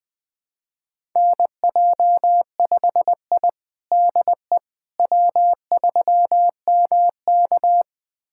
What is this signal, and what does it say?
Morse code, heard as a keyed tone, about 20 wpm, 700 Hz. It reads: NJ5I DE W3MK
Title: USA Callsign DE Callsign - 4 Characters - 20wpm